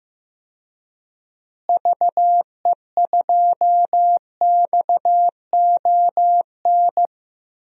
Morse code, heard as a keyed tone, about 15 wpm, 700 Hz. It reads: VE2XON
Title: DXCC-100 Call Signs - 6 Characters - Repeated 2x - 15wpm